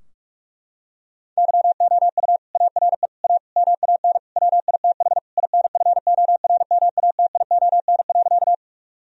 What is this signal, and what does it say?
Morse code, heard as a keyed tone, about 46 wpm, 700 Hz. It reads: YOU ARE A MAN WITH INFORMATION.
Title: Sentences from Top 1000 Words - Review Part 3 - 45wpm